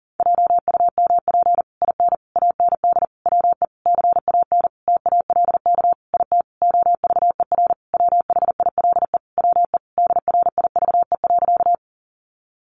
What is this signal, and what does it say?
Morse code, heard as a keyed tone, about 40 wpm, 700 Hz. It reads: JUMP IN AND WE CAN TALK IT OVER WHILE WE DRIVE.
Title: Sentences from Top 1300 Words 40wpm